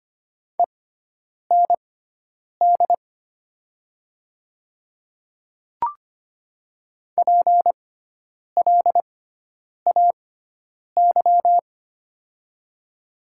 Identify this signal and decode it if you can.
Morse code, heard as a keyed tone, about 25 wpm, spaced as for 9 wpm, 700 Hz. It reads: END PLAY
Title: 6x Character Spacing - Top 200 Words - 25wpm